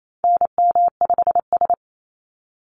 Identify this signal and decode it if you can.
Morse code, heard as a keyed tone, about 28 wpm, 700 Hz. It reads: NM5S
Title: NA SOTA Chaser Callsigns 28wpm